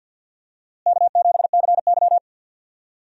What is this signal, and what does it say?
Morse code, heard as a keyed tone, about 53 wpm, 700 Hz. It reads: K7CY
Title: USA Call Signs - 4 Characters - Repeated 3x - 50wpm